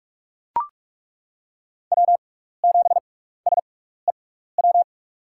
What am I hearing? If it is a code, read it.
Morse code, heard as a keyed tone, about 42 wpm, spaced as for 15 wpm, 700 Hz. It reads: W7SEW
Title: USA Call Signs - 6x Character Spacing - 45wpm